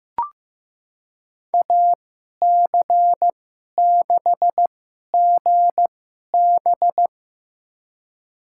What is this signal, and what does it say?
Morse code, heard as a keyed tone, about 16 wpm, spaced as for 11 wpm, 700 Hz. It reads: AC6GB
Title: USA Call Signs - 2x Character Spacing - 15wpm